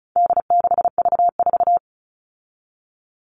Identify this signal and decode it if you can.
Morse code, heard as a keyed tone, about 34 wpm, 700 Hz. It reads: D6V4
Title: All Cut Numbers - 4-Digit 35wpm